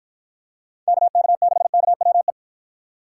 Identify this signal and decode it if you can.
Morse code, heard as a keyed tone, about 52 wpm, 700 Hz. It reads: KK6KWE